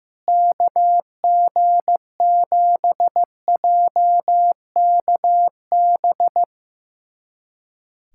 Morse code, heard as a keyed tone, about 15 wpm, 700 Hz. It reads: KG7JKB
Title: USA Callsign DE Callsign 15wpm